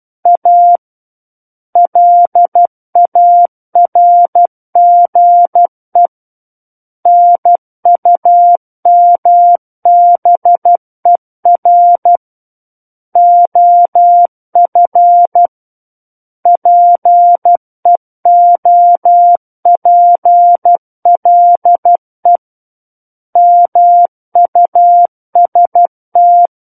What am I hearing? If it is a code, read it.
Morse code, heard as a keyed tone, about 12 wpm, 700 Hz. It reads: A LARGE NUMBER OF PEOPLE MUST
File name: Book One / 03-Chapter 3 - 12 WPM